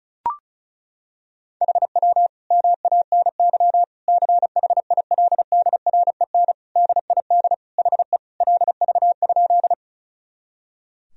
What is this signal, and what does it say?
Morse code, heard as a keyed tone, about 36 wpm, 700 Hz. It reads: HW MANY CHILDREN DID HE LV?
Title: Sentences from Top 400 Words and Abbreviations 35wpm